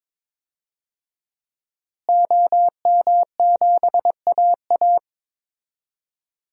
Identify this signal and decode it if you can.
Morse code, heard as a keyed tone, about 22 wpm, 700 Hz. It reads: OM7AA
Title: DXCC-100 Call Signs - 5 Characters - Repeated 3x - 22wpm